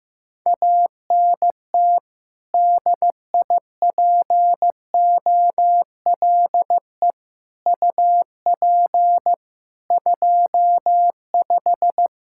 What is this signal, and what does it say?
Morse code, heard as a keyed tone, about 15 wpm, 700 Hz. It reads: ANT DIPOLE UP 25
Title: K1USN SST QSO Practice 15wpm